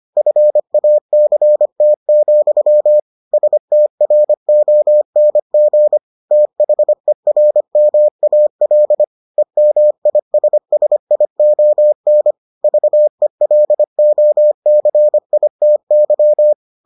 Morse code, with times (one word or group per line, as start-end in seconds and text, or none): 0.2-3.0 FACT,
3.3-6.0 STRONG
6.3-9.0 THERMAL
9.4-12.3 EMISSION
12.6-16.5 VELOCITY